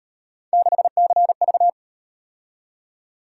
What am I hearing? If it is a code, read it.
Morse code, heard as a keyed tone, about 38 wpm, 700 Hz. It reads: 6CV